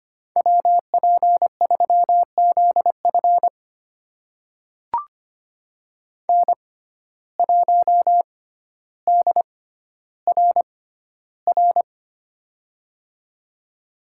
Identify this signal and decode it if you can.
Morse code, heard as a keyed tone, about 25 wpm, spaced as for 13 wpm, 700 Hz. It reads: WP3ZF N1DRR
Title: USA Call Signs - 5 Characters - 6x Character Spacing - 25wpm